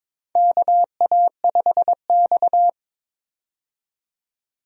Morse code, heard as a keyed tone, about 22 wpm, 700 Hz. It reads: KA5X
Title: USA Call Signs - 4 Characters - 22wpm